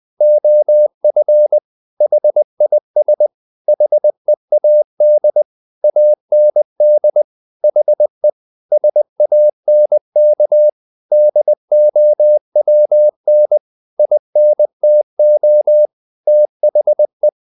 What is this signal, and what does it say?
Morse code, heard as a keyed tone, about 20 wpm, 600 Hz. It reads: OF HIS HEAD AND HE SANK DOWN INTO THE
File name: the old man and the sea part 2 20wpm